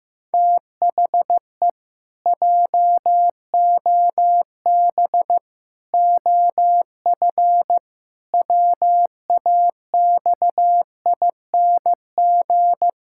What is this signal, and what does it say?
Morse code, heard as a keyed tone, about 14 wpm, 700 Hz. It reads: THE JOB OF WAXING
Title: Pangram Challenge v2 - 15wpm